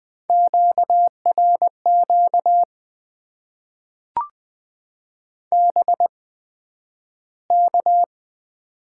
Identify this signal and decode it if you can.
Morse code, heard as a keyed tone, about 20 wpm, spaced as for 11 wpm, 700 Hz. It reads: QRQ BK
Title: Top 90 QSO Elements - 8x Character Spacing - 20wpm